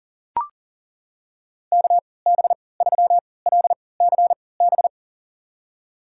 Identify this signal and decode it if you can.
Morse code, heard as a keyed tone, about 40 wpm, spaced as for 23 wpm, 700 Hz. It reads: KB3LCB
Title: USA Call Signs - 3x Character Spacing - 40wpm